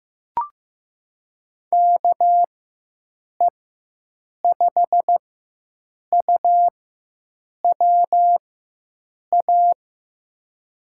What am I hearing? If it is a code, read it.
Morse code, heard as a keyed tone, about 15 wpm, spaced as for 7 wpm, 700 Hz. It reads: KE5UWA
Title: USA Call Signs - 6 Characters - 4x Character Spacing - 15wpm